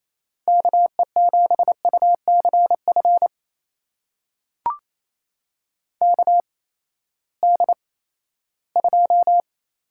Morse code, heard as a keyed tone, about 27 wpm, spaced as for 16 wpm, 700 Hz. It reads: KE7UCF KD2
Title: USA Call Signs - 6 Characters - 8x Character Spacing - 28wpm